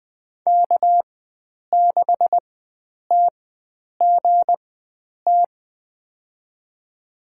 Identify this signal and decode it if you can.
Morse code, heard as a keyed tone, about 20 wpm, spaced as for 9 wpm, 700 Hz. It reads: K6TGT
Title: USA Call Signs - 5 Characters - 4x Character Spacing - 20wpm